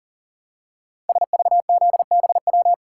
Morse code, heard as a keyed tone, about 40 wpm, 700 Hz. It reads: SVZBW
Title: Code Groups - 5 Letters - 40wpm